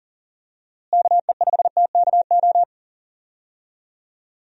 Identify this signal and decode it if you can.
Morse code, heard as a keyed tone, about 40 wpm, 700 Hz. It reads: KE5TKO
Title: USA Call Signs - 6 Characters - Repeated 2x - 40wpm